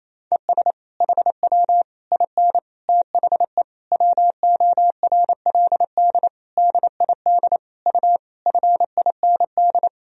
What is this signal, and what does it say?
Morse code, heard as a keyed tone, about 28 wpm, 700 Hz. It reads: ES HW IN THE WORLD DID U FIND
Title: Sentences from Top 200 Words and Abbreviations 28wpm